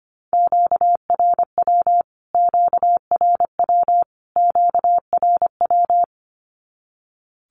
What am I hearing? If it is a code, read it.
Morse code, heard as a keyed tone, about 25 wpm, 700 Hz. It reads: QRW QRW QRW
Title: Q-Codes Repeated 3x 25wpm